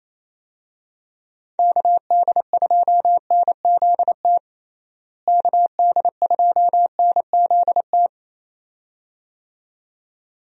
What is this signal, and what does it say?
Morse code, heard as a keyed tone, about 28 wpm, 700 Hz. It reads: KD2NZT KD2NZT
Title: USA Call Signs - 6 Characters - Repeated 2x - 28wpm